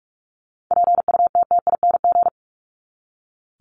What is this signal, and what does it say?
Morse code, heard as a keyed tone, about 44 wpm, 700 Hz. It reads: PUTTING
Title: ING Suffix - Top 100 Words with Base Words 45wpm